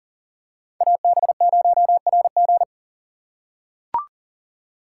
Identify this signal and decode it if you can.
Morse code, heard as a keyed tone, about 40 wpm, 700 Hz. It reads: AB0RG
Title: USA Call Signs - 5 Characters - 3x Character Spacing - 40wpm